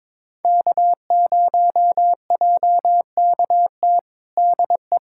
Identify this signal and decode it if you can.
Morse code, heard as a keyed tone, about 22 wpm, 700 Hz. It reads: K0JKT DE
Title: USA Callsign DE Callsign 22wpm